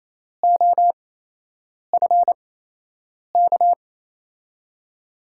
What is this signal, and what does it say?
Morse code, heard as a keyed tone, about 28 wpm, spaced as for 8 wpm, 700 Hz. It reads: OFK
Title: USA Call Signs - 8x Character Spacing - 28wpm